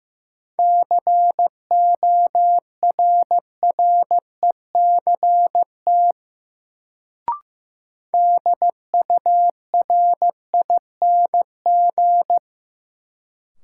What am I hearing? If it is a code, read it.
Morse code, heard as a keyed tone, about 15 wpm, 700 Hz. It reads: CORRECT DURING
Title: Sentences from Top 700 Words - New Word Drill - 15wpm